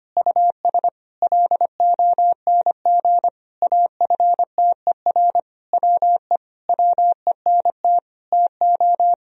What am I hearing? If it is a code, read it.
Morse code, heard as a keyed tone, about 25 wpm, 700 Hz. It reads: US LONG AFTER WE WENT TO
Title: Sentences from Top 1000 Words - Review Part 2 - 25wpm